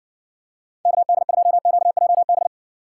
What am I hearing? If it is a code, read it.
Morse code, heard as a keyed tone, about 60 wpm, 700 Hz. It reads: KD2QJB